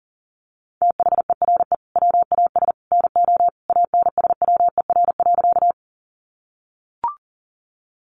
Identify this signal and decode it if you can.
Morse code, heard as a keyed tone, about 40 wpm, 700 Hz. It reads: THERE WAS NO ANSWER.
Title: Sentences from Top 300 Words 40wpm